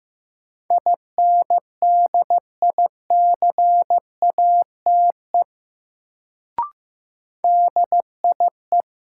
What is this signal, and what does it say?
Morse code, heard as a keyed tone, about 15 wpm, 700 Hz. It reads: INDICATE DIE
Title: Sentences from Top 1300 Words - New Word Drill - 15wpm